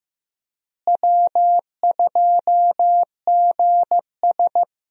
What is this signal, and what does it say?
Morse code, heard as a keyed tone, about 15 wpm, 700 Hz. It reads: W2GS
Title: USA Call Signs - 4 Characters - Repeated 2x - 15wpm